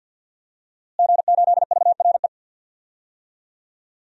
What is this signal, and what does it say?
Morse code, heard as a keyed tone, about 52 wpm, 700 Hz. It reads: G8VRE
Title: DXCC-100 Call Signs - 5 Characters - Repeated 3x - 50wpm